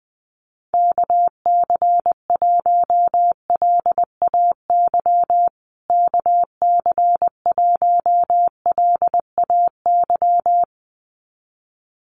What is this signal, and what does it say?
Morse code, heard as a keyed tone, about 20 wpm, 700 Hz. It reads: KC1LAY KC1LAY
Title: USA Call Signs Repeated 2x 20wpm